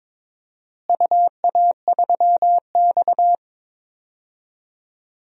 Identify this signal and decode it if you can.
Morse code, heard as a keyed tone, about 21 wpm, 700 Hz. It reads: UA3X
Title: DXCC-100 Call Signs - 4 Characters - Repeated 2x - 22wpm